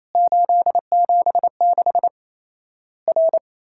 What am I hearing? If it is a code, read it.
Morse code, heard as a keyed tone, about 28 wpm, 700 Hz. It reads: 876 R
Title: Worked All Europe DX Contest - QTC Exchange - 28wpm